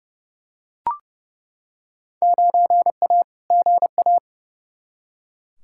Morse code, heard as a keyed tone, about 30 wpm, 700 Hz. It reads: 9A GA
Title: Field Day Exchanges Encore - 30wpm